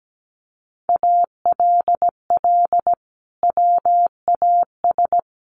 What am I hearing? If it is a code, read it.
Morse code, heard as a keyed tone, about 17 wpm, 700 Hz. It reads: ALL WAS
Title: Sentences from Top 500 Words 17wpm